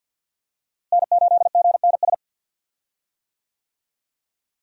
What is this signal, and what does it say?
Morse code, heard as a keyed tone, about 50 wpm, 700 Hz. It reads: N8GNS
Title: USA Call Signs - 5 Characters - Repeated 2x - 50wpm